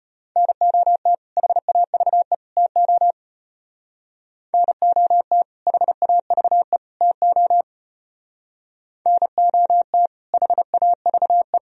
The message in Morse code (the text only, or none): NOT HAVE TO NOT HAVE TO NOT HAVE